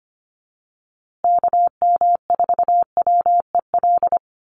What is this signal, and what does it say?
Morse code, heard as a keyed tone, about 25 wpm, 700 Hz. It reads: KM4WEL
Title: USA Call Signs Repeated 2x 25wpm